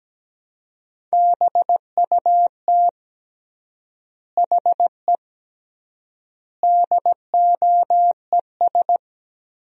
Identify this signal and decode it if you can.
Morse code, heard as a keyed tone, about 17 wpm, 700 Hz. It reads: BUT HE DOES
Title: 3x Word-Spacing - Sentences from Top 200 Words - 17wpm